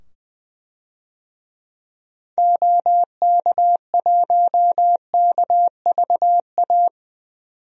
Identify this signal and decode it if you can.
Morse code, heard as a keyed tone, about 20 wpm, 700 Hz. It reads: OK1KVA
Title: DXCC-100 Call Signs - 6 Characters - 20wpm